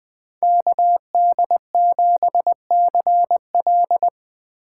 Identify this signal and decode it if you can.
Morse code, heard as a keyed tone, about 20 wpm, 700 Hz. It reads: KD7CL
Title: USA Call Signs - 5 Characters - 8x Character Spacing - 20wpm